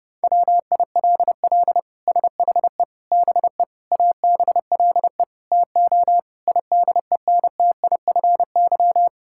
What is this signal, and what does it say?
Morse code, heard as a keyed tone, about 30 wpm, 700 Hz. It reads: WILL SHE BE ABLE TO IDENTIFY
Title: Sentences from Top 1000 Words - Review Part 6 - 30wpm